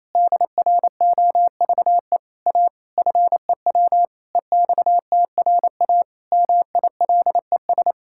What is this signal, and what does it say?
Morse code, heard as a keyed tone, about 28 wpm, 700 Hz. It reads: DROVE A FEW EXTRA MILES
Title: Pangram Challenge v2 - 28wpm